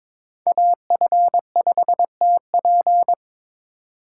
Morse code, heard as a keyed tone, about 21 wpm, 700 Hz. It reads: AF5TP
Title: USA Call Signs - 5 Characters - 2x Character Spacing - 22wpm